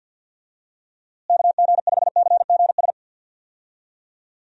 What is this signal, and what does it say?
Morse code, heard as a keyed tone, about 50 wpm, 700 Hz. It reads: KG5CGS
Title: USA Call Signs - 6 Characters - Repeated 3x - 50wpm